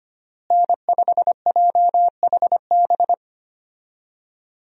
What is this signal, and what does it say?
Morse code, heard as a keyed tone, about 25 wpm, 700 Hz. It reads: N5JHB
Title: USA Call Signs - 5 Characters - Repeated 3x - 25wpm